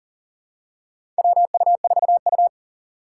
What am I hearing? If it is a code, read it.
Morse code, heard as a keyed tone, about 39 wpm, 700 Hz. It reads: WU4U
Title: USA Call Signs - 3x Character Spacing - 40wpm